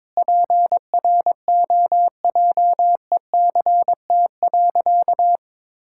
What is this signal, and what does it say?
Morse code, heard as a keyed tone, about 22 wpm, 700 Hz. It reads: PROJECT.
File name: Sentences with 9-Word Phrases - 22wpm